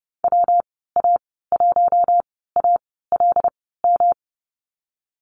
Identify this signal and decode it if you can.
Morse code, heard as a keyed tone, about 30 wpm, spaced as for 17 wpm, 700 Hz. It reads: WA1ALM
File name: Lesson 19g - Letter w - Call Signs 30wpm